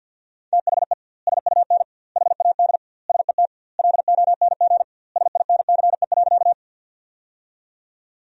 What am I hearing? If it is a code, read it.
Morse code, heard as a keyed tone, about 49 wpm, 700 Hz. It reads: THE SUN HAD SET LONG SINCE.
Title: 2x Word-Spacing - Sentences from Top 300 Words - 50wpm